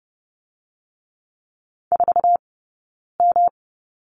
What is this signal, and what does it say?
Morse code, heard as a keyed tone, about 30 wpm, 700 Hz. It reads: 4 M
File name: Lesson 18d - Letter m - Two Characters 30wpm